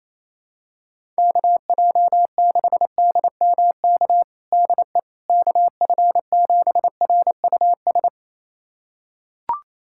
Morse code, heard as a keyed tone, about 28 wpm, 700 Hz. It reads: KJ6DMK DE KF7RUS
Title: USA Callsign DE Callsign - 6 Characters - 28wpm